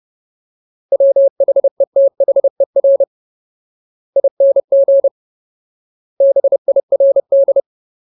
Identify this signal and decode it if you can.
Morse code, heard as a keyed tone, about 30 wpm, 550 Hz. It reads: WHETHER ING BIRD
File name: Ultimate Word List / 30WPM Ultimate